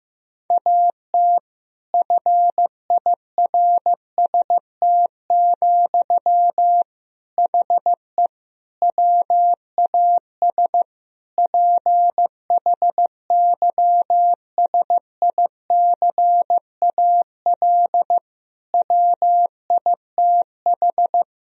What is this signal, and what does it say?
Morse code, heard as a keyed tone, about 15 wpm, 700 Hz. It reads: AT FIRST, HE WAS PHYSICAL WITH
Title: Sentences from Top 1000 Words - Review Part 4 - 15wpm